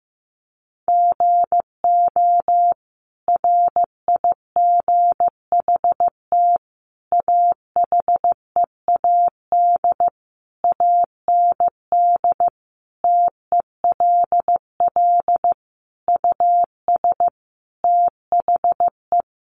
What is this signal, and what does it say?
Morse code, heard as a keyed tone, about 15 wpm, 700 Hz. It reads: GO RIGHT AHEAD AND TELL US THE
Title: Sentences from Top 1100 Words 15wpm